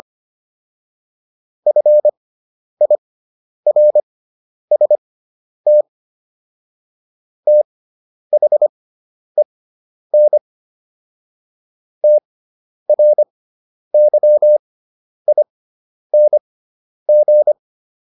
Morse code, heard as a keyed tone, about 25 wpm, spaced as for 10 wpm, 600 Hz. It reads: FIRST THEN TRYING